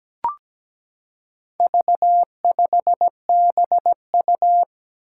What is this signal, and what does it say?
Morse code, heard as a keyed tone, about 17 wpm, 700 Hz. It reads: V5BU
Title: Four Challenging Letter-Number Combinations 17wpm